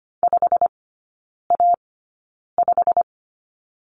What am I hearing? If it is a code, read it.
Morse code, heard as a keyed tone, about 25 wpm, spaced as for 9 wpm, 690 Hz. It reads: 5A5
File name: Lesson 15e - Number 5 - Three Characters 25wpm